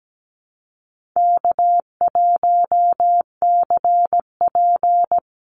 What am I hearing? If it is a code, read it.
Morse code, heard as a keyed tone, about 17 wpm, 700 Hz. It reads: K1CP
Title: Call Signs - 17wpm